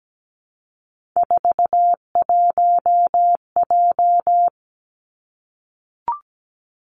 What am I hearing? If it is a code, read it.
Morse code, heard as a keyed tone, about 17 wpm, 700 Hz. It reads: 41J
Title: Three Challenging Letter-Number Combinations 17wpm